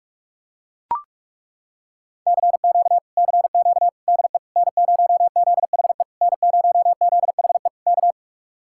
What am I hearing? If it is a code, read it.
Morse code, heard as a keyed tone, about 40 wpm, 700 Hz. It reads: CQ CQ DE N0ZHE N0ZHE K